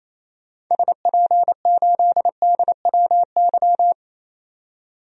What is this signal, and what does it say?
Morse code, heard as a keyed tone, about 28 wpm, 700 Hz. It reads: SP8DWY
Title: DXCC-100 Call Signs - 6 Characters - 28wpm